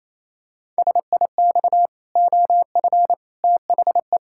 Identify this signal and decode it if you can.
Morse code, heard as a keyed tone, about 28 wpm, 700 Hz. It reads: SIX OF THE
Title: Pangram Challenge v2 - 28wpm